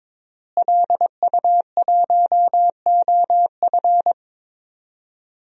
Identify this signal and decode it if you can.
Morse code, heard as a keyed tone, about 22 wpm, 700 Hz. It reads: LU1OF
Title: DXCC-100 Call Signs - 5 Characters - Repeated 3x - 22wpm